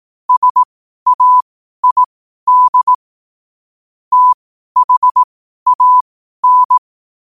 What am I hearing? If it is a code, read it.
Morse code, heard as a keyed tone, about 18 wpm, spaced as for 12 wpm, 1000 Hz. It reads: SAID THAN
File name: qotdfiles / 12wpm